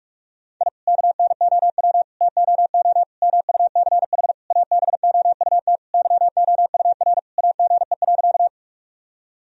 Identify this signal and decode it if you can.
Morse code, heard as a keyed tone, about 46 wpm, 700 Hz. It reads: I KNOW TOO MUCH ABOUT YOUR AGE.
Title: Sentences from Top 1000 Words - Review Part 3 - 45wpm